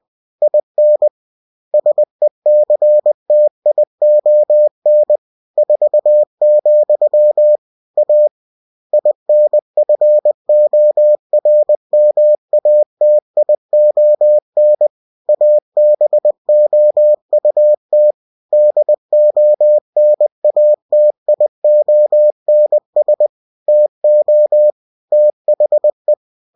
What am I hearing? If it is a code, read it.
Morse code, heard as a keyed tone, about 20 wpm, 600 Hz. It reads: IN SECTION 4, A INFORMATION ABOUT DONATIONS TO THE
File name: cw-books / winnie-the-pooh / winnie-20wpm-page-0117